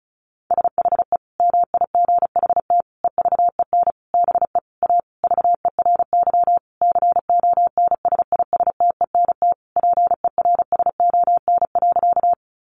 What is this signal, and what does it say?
Morse code, heard as a keyed tone, about 34 wpm, 700 Hz. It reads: SHE MIGHT EVEN BE A VERY CONSISTENT PERSON.
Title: Sentences from Top 1500 Words 35wpm